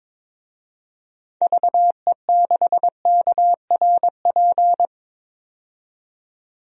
Morse code, heard as a keyed tone, about 22 wpm, 700 Hz. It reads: VE6KRP
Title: DXCC-100 Call Signs - 6 Characters - 22wpm